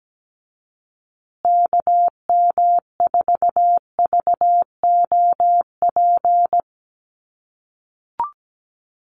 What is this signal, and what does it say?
Morse code, heard as a keyed tone, about 17 wpm, 700 Hz. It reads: KM4VOP
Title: USA Call Signs Repeated 2x 17wpm